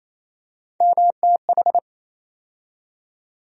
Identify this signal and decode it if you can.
Morse code, heard as a keyed tone, about 28 wpm, 700 Hz. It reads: MTH
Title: USA Call Sign Postfixes - 3 Letters 28wpm